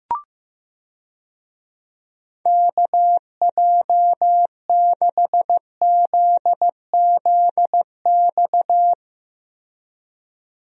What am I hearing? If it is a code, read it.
Morse code, heard as a keyed tone, about 15 wpm, 700 Hz. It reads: KJ6ZZX
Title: USA Call Signs - 6 Characters - Repeated 2x - 15wpm